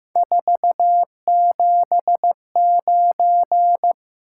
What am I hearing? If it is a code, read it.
Morse code, heard as a keyed tone, about 15 wpm, 700 Hz. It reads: 479
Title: Top 180 QSO Elements 15wpm